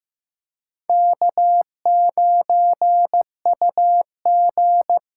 Morse code, heard as a keyed tone, about 15 wpm, 700 Hz. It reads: K9UG
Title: USA Call Signs - 2x Character Spacing - 15wpm